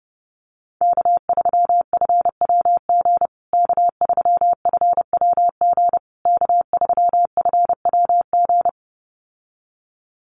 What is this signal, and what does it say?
Morse code, heard as a keyed tone, about 30 wpm, 700 Hz. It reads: K3FWG K3FWG K3FWG
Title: USA Call Signs Repeated 3x 30wpm